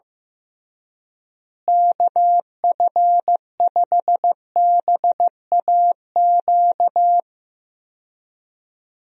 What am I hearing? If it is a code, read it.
Morse code, heard as a keyed tone, about 15 wpm, 700 Hz. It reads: KF5BAQ
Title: USA Call Signs - 6 Characters - Repeated 2x - 15wpm